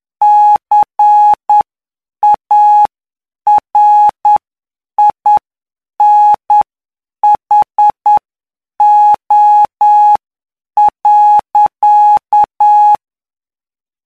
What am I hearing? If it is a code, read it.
Morse code, heard as a keyed tone, about 9 wpm, spaced as for 7 wpm, 840 Hz. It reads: CARINHO.
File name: cw / py2lcd / 18-PY2LCD-Li��o 27